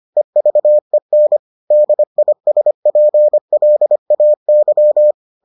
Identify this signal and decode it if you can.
Morse code, heard as a keyed tone, about 25 wpm, 600 Hz. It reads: EVEN DISPLAY